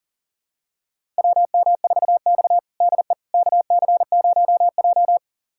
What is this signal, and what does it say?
Morse code, heard as a keyed tone, about 37 wpm, 700 Hz. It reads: WM4X DE KC0J
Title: USA Callsign DE Callsign - 4 Characters - 40wpm